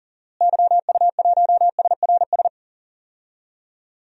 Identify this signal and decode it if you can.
Morse code, heard as a keyed tone, about 40 wpm, 700 Hz. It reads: YU1SRS
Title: DXCC-100 Call Signs - 6 Characters - Repeated 2x - 40wpm